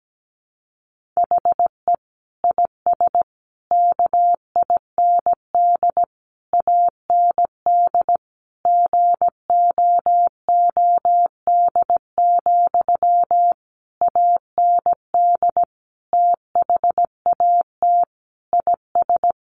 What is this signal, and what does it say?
Morse code, heard as a keyed tone, about 17 wpm, 700 Hz. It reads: HE IS KIND AND GOOD, AND THAT IS
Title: Sentences from Top 1400 Words 17wpm